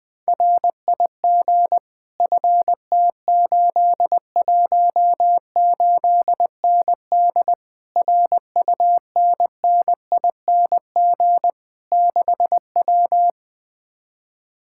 Morse code, heard as a keyed tone, about 20 wpm, 700 Hz. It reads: RIG FT818ND RUNNING 6W
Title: Speed-Racing Common QSO Phrases Encore - 17wpm